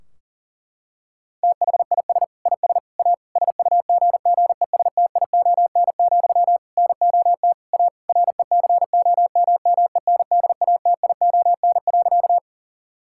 Morse code, heard as a keyed tone, about 37 wpm, 700 Hz. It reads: THIS IS A SUGGESTION, NOT A RECOMMENDATION.
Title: Sentences from Top 1600 Words 40wpm